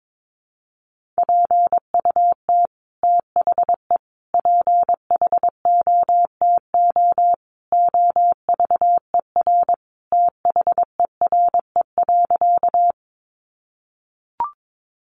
Morse code, heard as a keyed tone, about 22 wpm, 700 Hz. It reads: PUT THE PHOTO OVER THERE.
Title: Sentences from Top 1200 Words 22wpm